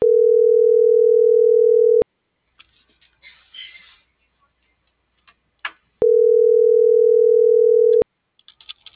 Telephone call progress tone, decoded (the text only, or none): ringback tone